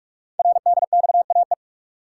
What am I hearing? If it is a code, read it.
Morse code, heard as a keyed tone, about 44 wpm, 700 Hz. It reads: RDXAE
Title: Code Groups - 5 Letters - 45wpm